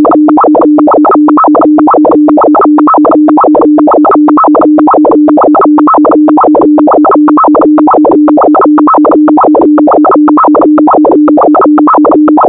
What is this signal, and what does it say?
Two-tone teleprinter signal, about 450 Hz shift.
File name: thet / radia297 / radia s24 n297 radiohelsinki-raggam--dataradio-pt4